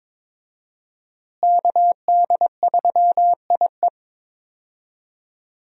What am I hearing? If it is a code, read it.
Morse code, heard as a keyed tone, about 22 wpm, 700 Hz. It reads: KD3IE
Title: USA Call Signs - 5 Characters - Repeated 3x - 22wpm